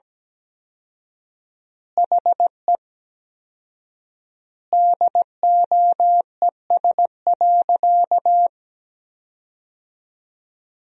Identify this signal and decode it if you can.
Morse code, heard as a keyed tone, about 17 wpm, 700 Hz. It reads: HE DOES.
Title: 4x Word-Spacing - Sentences from Top 500 Words - 17wpm